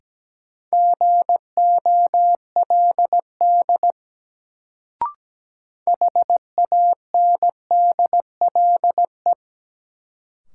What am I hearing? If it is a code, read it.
Morse code, heard as a keyed tone, about 17 wpm, 700 Hz. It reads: GOLD HANDLE
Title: Sentences from Top 1100 Words - New Word Drill Alt - 17wpm